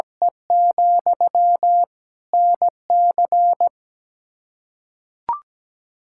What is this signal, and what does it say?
Morse code, heard as a keyed tone, about 17 wpm, 700 Hz. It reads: E, NC
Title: K1USN SST QSO Practice 17wpm